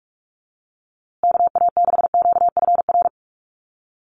Morse code, heard as a keyed tone, about 44 wpm, 700 Hz. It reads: KA6QFR
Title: USA Call Signs Repeated 3x 45wpm